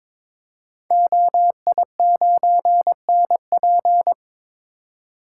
Morse code, heard as a keyed tone, about 22 wpm, 700 Hz. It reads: OI9NP
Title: Code Groups - 5 Letter:Numbers - 22wpm